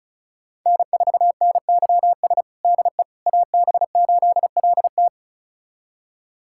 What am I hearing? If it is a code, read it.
Morse code, heard as a keyed tone, about 36 wpm, 700 Hz. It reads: N4NYS DE AB8LT
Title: USA Callsign DE Callsign - 5 Characters - 35wpm